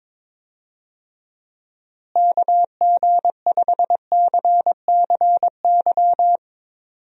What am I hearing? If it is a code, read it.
Morse code, heard as a keyed tone, about 22 wpm, 700 Hz. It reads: KG5CCY
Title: USA Call Signs - 6 Characters - Repeated 2x - 22wpm